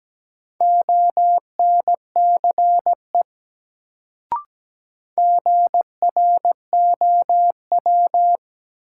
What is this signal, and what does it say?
Morse code, heard as a keyed tone, about 17 wpm, 700 Hz. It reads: ONCE GROW
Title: Sentences from Top 300 Words - New Word Drill - 17wpm